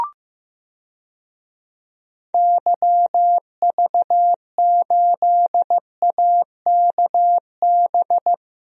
Morse code, heard as a keyed tone, about 15 wpm, 700 Hz. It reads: YV8AKB
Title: DXCC-100 Call Signs - 6 Characters - Repeated 3x - 15wpm